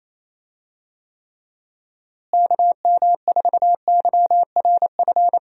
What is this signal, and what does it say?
Morse code, heard as a keyed tone, about 27 wpm, 700 Hz. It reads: KM4YRF
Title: USA Call Signs - 6 Characters - Repeated 3x - 28wpm